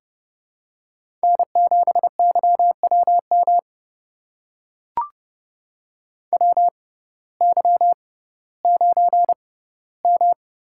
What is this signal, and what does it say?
Morse code, heard as a keyed tone, about 31 wpm, spaced as for 17 wpm, 700 Hz. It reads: N7YWM WY9M
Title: USA Call Signs - 6x Character Spacing - 30wpm